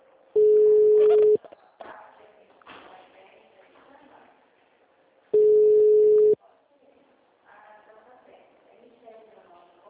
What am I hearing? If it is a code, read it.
Ringback tone (Europe, 425 Hz), with speech.
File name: out- -2013- - - .74